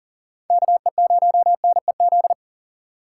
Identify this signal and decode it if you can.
Morse code, heard as a keyed tone, about 48 wpm, 700 Hz. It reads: KE0NEZ
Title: Speed Racing - USA 2x3 Call Signs 40wpm